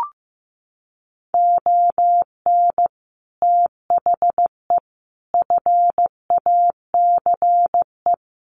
Morse code, heard as a keyed tone, about 15 wpm, 700 Hz. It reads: ON THE FACE
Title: Sets of 5 Words Encore2 15wpm